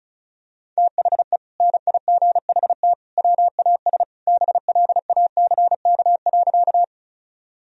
Morse code, heard as a keyed tone, about 34 wpm, 700 Hz. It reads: THE NIGHT WAS BLACK.
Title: Sentences from Top 1000 Words - Review Part 2 - 35wpm